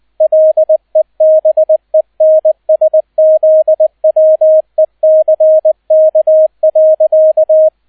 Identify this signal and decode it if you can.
Morse code, heard as a keyed tone, about 20 wpm, 630 Hz, with radio noise. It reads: LEBENSZWECK.